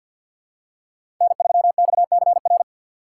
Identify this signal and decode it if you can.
Morse code, heard as a keyed tone, about 48 wpm, 700 Hz. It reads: N3XCR
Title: USA Call Signs - 5 Characters - 6x Character Spacing - 50wpm